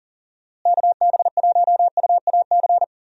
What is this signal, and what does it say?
Morse code, heard as a keyed tone, about 40 wpm, 700 Hz. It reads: KB1UAC
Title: USA Call Signs - 6 Characters - Repeated 3x - 40wpm